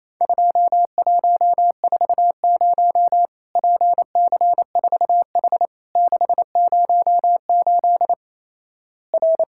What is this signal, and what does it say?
Morse code, heard as a keyed tone, about 27 wpm, 700 Hz. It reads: 2140 PC4H 608 R
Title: Worked All Europe DX Contest - QTC Exchange - 28wpm